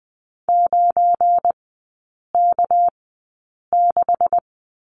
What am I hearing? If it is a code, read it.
Morse code, heard as a keyed tone, about 20 wpm, spaced as for 8 wpm, 700 Hz. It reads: 9K6
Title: 218 Lesson 35e - Number 0 - Three Characters 20wpm